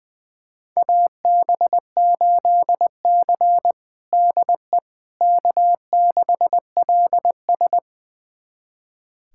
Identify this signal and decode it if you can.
Morse code, heard as a keyed tone, about 20 wpm, 700 Hz. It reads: AB8C DE K6LS
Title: USA Callsign DE Callsign - 4 Characters - 20wpm